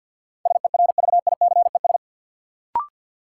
Morse code, heard as a keyed tone, about 49 wpm, 700 Hz. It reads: SERVICES
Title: Sets of 5 Words - New Word Drill - 50wpm